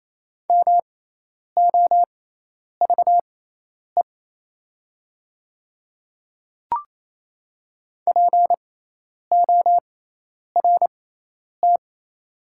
Morse code, heard as a keyed tone, about 28 wpm, spaced as for 10 wpm, 700 Hz. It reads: MOVE PORT
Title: 6x Character Spacing - Top 200 Words - 28wpm